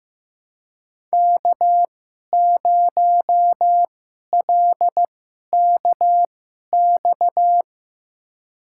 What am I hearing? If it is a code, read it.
Morse code, heard as a keyed tone, about 15 wpm, spaced as for 11 wpm, 700 Hz. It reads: K0LKX
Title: USA Call Signs - 5 Characters - 2x Character Spacing - 15wpm